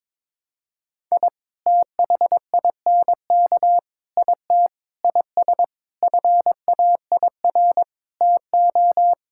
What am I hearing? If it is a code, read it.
Morse code, heard as a keyed tone, about 22 wpm, 700 Hz. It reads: I THINK IT IS FAIR TO
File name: practice / 22wpm / Sentences with 8-Word Phrases - 22wpm